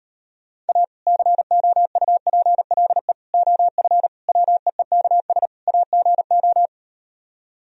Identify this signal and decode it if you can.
Morse code, heard as a keyed tone, about 38 wpm, 700 Hz. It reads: A COUPLE OF WEEKS AGO